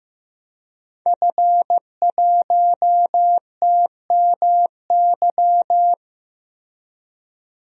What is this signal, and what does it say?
Morse code, heard as a keyed tone, about 15 wpm, 700 Hz. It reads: F1TMY
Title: DXCC-100 Call Signs - 5 Characters - 15wpm